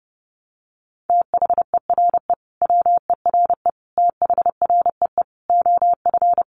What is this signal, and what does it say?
Morse code, heard as a keyed tone, about 30 wpm, 700 Hz. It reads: THERE WERE THREE OF
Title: Sentences from Top 800 Words 30wpm